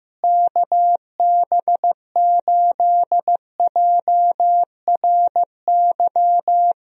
Morse code, heard as a keyed tone, about 15 wpm, 700 Hz. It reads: KB8JRY
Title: USA Call Signs - 6 Characters - Repeated 2x - 15wpm